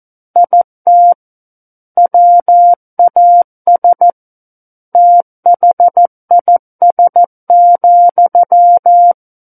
Morse code, heard as a keyed tone, about 14 wpm, 700 Hz. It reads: IT WAS THIS,
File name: Book One / 05-Chapter 5 - 14 WPM